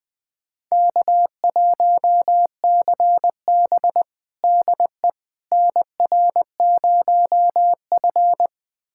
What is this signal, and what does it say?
Morse code, heard as a keyed tone, about 20 wpm, 700 Hz. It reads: K1CB DE NR0F
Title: USA Callsign DE Callsign - 4 Characters - 20wpm